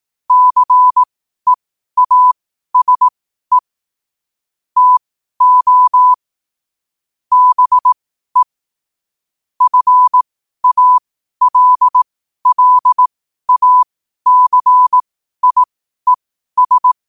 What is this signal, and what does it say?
Morse code, heard as a keyed tone, about 18 wpm, spaced as for 12 wpm, 1000 Hz. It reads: CEASE TO BE FALLACIES
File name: qotdfiles / 12wpm